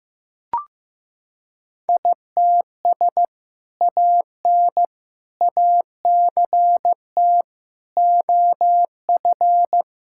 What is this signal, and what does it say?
Morse code, heard as a keyed tone, about 15 wpm, 700 Hz. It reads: ITS AN ACT OF
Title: Sentences from Top 1000 Words - Review Part 5 - 15wpm